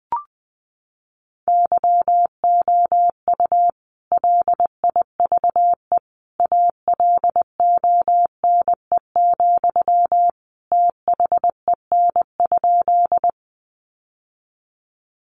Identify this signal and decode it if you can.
Morse code, heard as a keyed tone, about 20 wpm, 700 Hz. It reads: YOU LIVE ALONE, THEN?
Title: Sentences from Top 1000 Words 20wpm